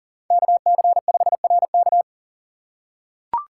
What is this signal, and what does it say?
Morse code, heard as a keyed tone, about 40 wpm, 700 Hz. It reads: KC5RK
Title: USA Call Signs - 5 Characters - 6x Character Spacing - 40wpm